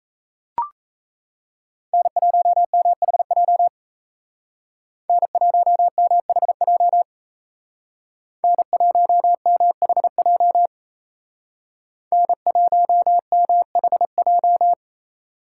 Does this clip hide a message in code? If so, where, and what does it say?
Yes: Morse code from 1.9 to 14.7 s, heard as a keyed tone, about 36 wpm, 700 Hz. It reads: N1MHJ N1MHJ N1MHJ N1MHJ